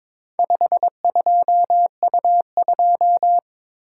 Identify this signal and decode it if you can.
Morse code, heard as a keyed tone, about 22 wpm, 700 Hz. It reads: 52U2
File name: practice / 22wpm / Four Letter-Number Combinations - 22wpm